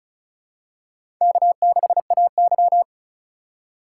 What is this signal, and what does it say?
Morse code, heard as a keyed tone, about 34 wpm, 700 Hz. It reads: K6AY
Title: USA Call Signs - 8x Character Spacing - 35wpm